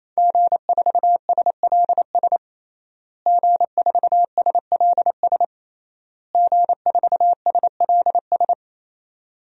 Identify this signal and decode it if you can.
Morse code, heard as a keyed tone, about 28 wpm, 700 Hz. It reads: G4SLS G4SLS G4SLS
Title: DXCC-100 Call Signs - 5 Characters - Repeated 3x - 28wpm